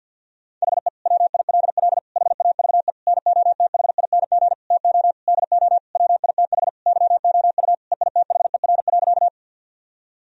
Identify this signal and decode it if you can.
Morse code, heard as a keyed tone, about 50 wpm, 700 Hz. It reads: HE WILL HAVE NOTHING TO DO WITH YOU EITHER.
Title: Sentences from Top 1000 Words - Review Part 4 - 50wpm